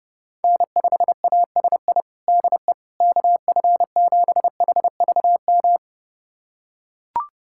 Morse code, heard as a keyed tone, about 32 wpm, 700 Hz. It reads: N5ASI DE KF7HVM
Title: USA Callsign DE Callsign 30wpm